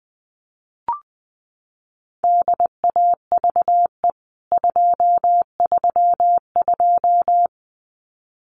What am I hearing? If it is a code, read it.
Morse code, heard as a keyed tone, about 19 wpm, 700 Hz. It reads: DAVE 232
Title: CWT Exchange 20wpm